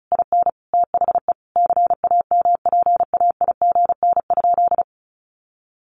Morse code, heard as a keyed tone, about 34 wpm, 700 Hz. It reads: IN THE CAMPAIGN?
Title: Sentences from Top 1200 Words 35wpm